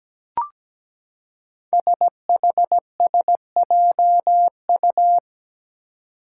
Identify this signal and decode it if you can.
Morse code, heard as a keyed tone, about 17 wpm, 700 Hz. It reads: SHSJU
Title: Five Challenging Letter-Number Combinations 17wpm